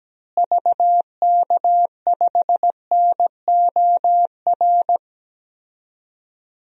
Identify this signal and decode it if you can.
Morse code, heard as a keyed tone, about 17 wpm, 700 Hz. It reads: VK5NOR
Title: DXCC-100 Call Signs - 6 Characters - Repeated 3x - 17wpm